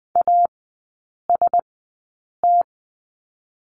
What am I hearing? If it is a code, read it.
Morse code, heard as a keyed tone, about 20 wpm, spaced as for 8 wpm, 700 Hz. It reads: AST
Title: 041 Lesson 8e - Number 1 - Three Characters 20wpm